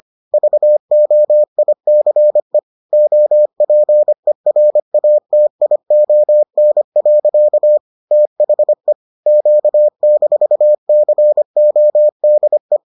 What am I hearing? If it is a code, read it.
Morse code, heard as a keyed tone, about 25 wpm, 600 Hz. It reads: VOICE OPERATION. THE Q-CODE